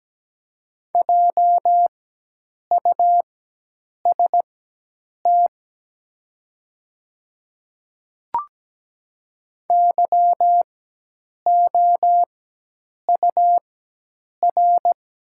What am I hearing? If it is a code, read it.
Morse code, heard as a keyed tone, about 17 wpm, spaced as for 8 wpm, 700 Hz. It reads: JUST YOUR
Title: 4x Character Spacing - Top 300 Words - 17wpm